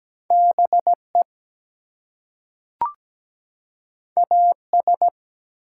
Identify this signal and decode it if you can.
Morse code, heard as a keyed tone, about 17 wpm, 700 Hz. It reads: BE AS
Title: Two Letter Words - 17wpm